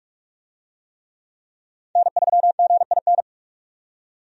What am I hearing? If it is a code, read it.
Morse code, heard as a keyed tone, about 46 wpm, 700 Hz. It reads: N3GIN
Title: USA Call Signs - 5 Characters - 45wpm